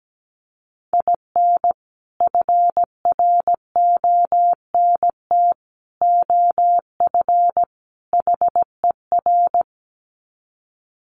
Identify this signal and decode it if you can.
Morse code, heard as a keyed tone, about 17 wpm, 700 Hz. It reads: IN FRONT OF HER
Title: Sets of 4 Words 17wpm